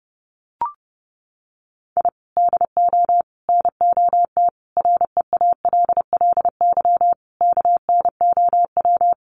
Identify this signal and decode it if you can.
Morse code, heard as a keyed tone, about 30 wpm, 700 Hz. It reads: I DO NOT REALLY KNOW
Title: Sets of 5 Words Encore 30wpm